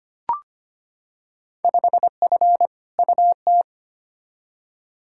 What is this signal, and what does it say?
Morse code, heard as a keyed tone, about 25 wpm, 700 Hz. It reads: 5F UT
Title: Field Day Exchanges - 25wpm